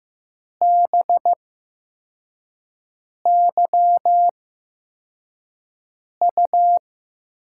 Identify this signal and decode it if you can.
Morse code, heard as a keyed tone, about 15 wpm, spaced as for 4 wpm, 700 Hz. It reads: BYU
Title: USA Call Signs - 6 Characters - 8x Character Spacing - 15wpm